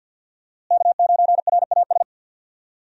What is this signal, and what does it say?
Morse code, heard as a keyed tone, about 52 wpm, 700 Hz. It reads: K9RAS
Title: USA Call Signs - 5 Characters - Repeated 2x - 50wpm